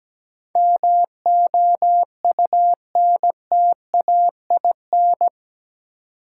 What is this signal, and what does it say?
Morse code, heard as a keyed tone, about 17 wpm, 700 Hz. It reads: MOUNTAIN
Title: Sentences from Top 1000 Words - Review Part 1 - New Word Drill - 17wpm